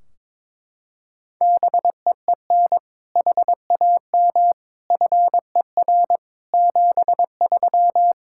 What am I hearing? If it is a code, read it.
Morse code, heard as a keyed tone, about 22 wpm, 700 Hz. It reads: BEEN HAM FER 73
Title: Practice QSO - Exchange 3 - More Info - 22wpm